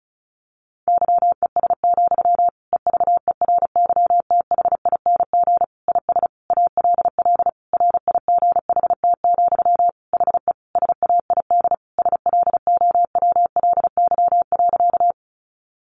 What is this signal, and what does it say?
Morse code, heard as a keyed tone, about 34 wpm, 700 Hz. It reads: YES, EVERYTHING IS ALL RIGHT, HE SAID SLOWLY.
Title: Sentences from Top 1100 Words 35wpm